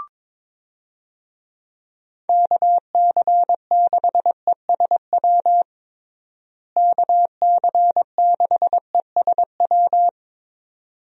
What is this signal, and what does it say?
Morse code, heard as a keyed tone, about 22 wpm, 700 Hz. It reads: KC6ESW KC6ESW
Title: USA Call Signs - 6 Characters - Repeated 2x - 22wpm